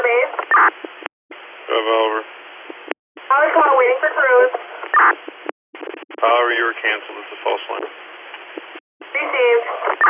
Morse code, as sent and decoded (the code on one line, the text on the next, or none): none